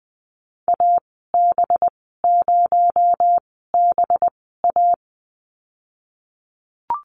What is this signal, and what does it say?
Morse code, heard as a keyed tone, about 21 wpm, spaced as for 15 wpm, 700 Hz. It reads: AB0BA
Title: 238 Lesson 38g - Letter q - Call Signs 20wpm